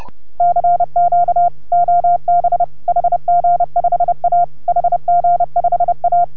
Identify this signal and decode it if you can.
Morse code, heard as a keyed tone, about 31 wpm, 690 Hz. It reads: CQ OB HG5A HG5A